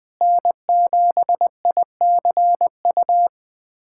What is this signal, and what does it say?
Morse code, heard as a keyed tone, about 20 wpm, 700 Hz. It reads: N7ICU
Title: USA Callsign DE Callsign 20wpm